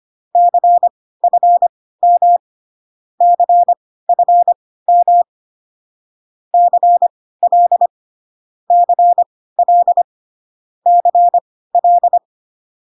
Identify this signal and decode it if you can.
Morse code, heard as a keyed tone, about 25 wpm, spaced as for 16 wpm, 700 Hz. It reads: CFM CFM CL CL CL